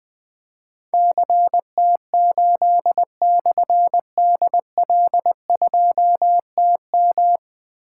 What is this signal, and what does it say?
Morse code, heard as a keyed tone, about 20 wpm, 700 Hz. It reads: CT8/DL2TM
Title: Difficult Call Signs 20wpm